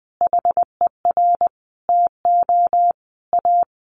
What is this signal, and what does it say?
Morse code, heard as a keyed tone, about 20 wpm, 700 Hz. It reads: HER TO A
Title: Easy Phrases 20wpm